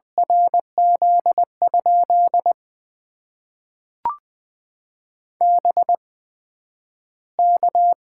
Morse code, heard as a keyed tone, about 20 wpm, spaced as for 11 wpm, 700 Hz. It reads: RZ? BK
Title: Top 90 QSO Elements - 8x Character Spacing - 20wpm